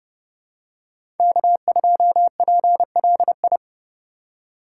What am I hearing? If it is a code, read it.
Morse code, heard as a keyed tone, about 30 wpm, 700 Hz. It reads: K2PLI
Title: USA Call Signs - 5 Characters - 4x Character Spacing - 30wpm